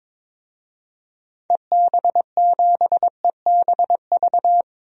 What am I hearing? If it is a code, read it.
Morse code, heard as a keyed tone, about 22 wpm, 700 Hz. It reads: EB7EBV
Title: DXCC-100 Call Signs - 6 Characters - 22wpm